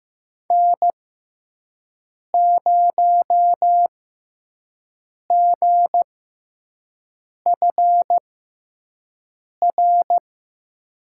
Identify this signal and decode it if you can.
Morse code, heard as a keyed tone, about 15 wpm, spaced as for 5 wpm, 700 Hz. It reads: N0GFR
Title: USA Call Signs - 5 Characters - 6x Character Spacing - 15wpm